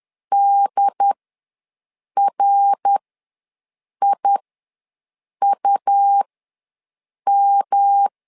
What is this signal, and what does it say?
Morse code, heard as a keyed tone, about 10 wpm, spaced as for 6 wpm, 800 Hz. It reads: DRIUM
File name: LEZIONI / 3 MWRO / 3 MWR ws 10wpm 2151